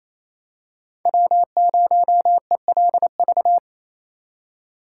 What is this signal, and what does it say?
Morse code, heard as a keyed tone, about 28 wpm, 700 Hz. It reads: W0ELV
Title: Code Groups - 5 Letter:Numbers - 28wpm